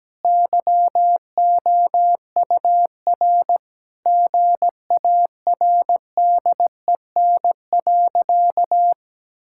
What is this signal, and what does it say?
Morse code, heard as a keyed tone, about 17 wpm, 700 Hz. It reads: YOUR GARDEN.
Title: Sentences from Top 1000 Words - Review Part 5 - 17wpm